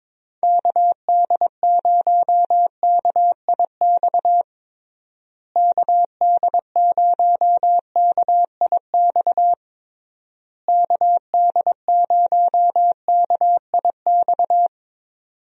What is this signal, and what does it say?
Morse code, heard as a keyed tone, about 22 wpm, 700 Hz. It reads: KD0KIX KD0KIX KD0KIX
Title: USA Call Signs - 6 Characters - Repeated 3x - 22wpm